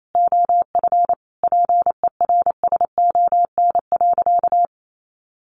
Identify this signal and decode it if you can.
Morse code, heard as a keyed tone, about 28 wpm, 700 Hz. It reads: OF PERSON.
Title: Sentences from Top 1400 Words 28wpm